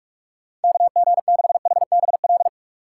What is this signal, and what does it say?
Morse code, heard as a keyed tone, about 46 wpm, 700 Hz. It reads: KG6HBL
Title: USA Call Signs - 6 Characters - 6x Character Spacing - 45wpm